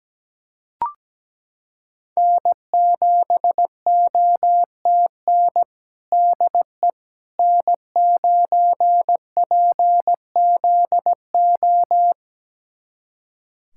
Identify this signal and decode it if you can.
Morse code, heard as a keyed tone, about 17 wpm, 700 Hz. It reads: N7OTN DE N9PZO
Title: USA Callsign DE Callsign - 5 Characters - 17wpm